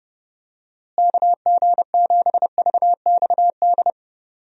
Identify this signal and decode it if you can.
Morse code, heard as a keyed tone, about 32 wpm, 700 Hz. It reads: KG7VXD
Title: USA Call Signs - 6 Characters - Repeated 3x - 30wpm